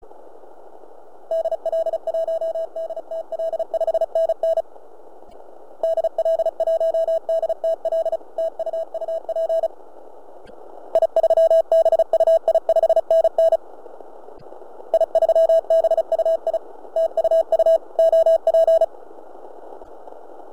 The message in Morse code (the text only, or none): DL1DTL 5NN DL1DTL TU UP I3BUI 5NN I3BUI TU U OP